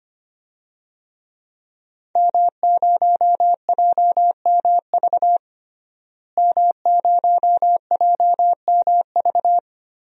Morse code, heard as a keyed tone, about 25 wpm, 700 Hz. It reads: M0JMV M0JMV
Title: DXCC-100 Call Signs - 5 Characters - Repeated 3x - 25wpm